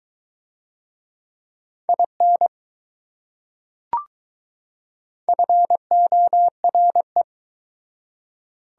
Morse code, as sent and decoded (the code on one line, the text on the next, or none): .. -. / ..-. --- .-. .
IN FORE